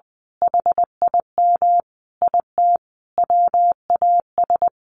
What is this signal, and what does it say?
Morse code, heard as a keyed tone, about 20 wpm, 700 Hz. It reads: HIM IT WAS
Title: A Sound Of Thunder 20wpm